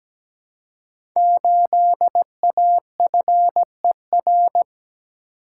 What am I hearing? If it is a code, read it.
Morse code, heard as a keyed tone, about 17 wpm, 700 Hz. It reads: 8AFER
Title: Code Groups - 5 Letter:Numbers - 17wpm